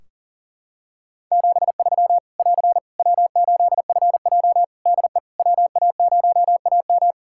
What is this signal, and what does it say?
Morse code, heard as a keyed tone, about 40 wpm, 700 Hz. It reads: 73 <AR> W8FJ DE WA0AM
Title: Practice QSO - Exchange 4 - Ending - 40wpm